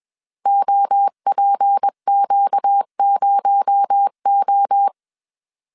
Morse code, heard as a keyed tone, about 21 wpm, 800 Hz. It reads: OPQ0O
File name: CW / DET / 1 6 1500C S B 20wpm 2048